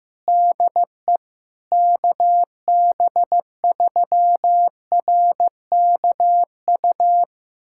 Morse code, heard as a keyed tone, about 16 wpm, 700 Hz. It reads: DE KB3RKU
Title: USA Callsign DE Callsign 15wpm